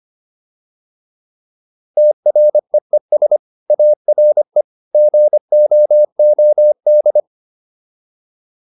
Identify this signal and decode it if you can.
Morse code, heard as a keyed tone, about 25 wpm, 600 Hz. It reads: TREES ARE GOOD
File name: PR405 25